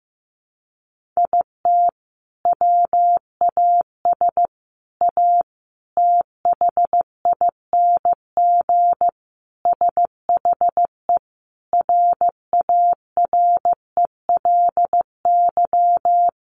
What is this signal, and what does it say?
Morse code, heard as a keyed tone, about 15 wpm, 700 Hz. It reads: IT WAS A THING SHE RARELY
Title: Sentences from Top 1400 Words 15wpm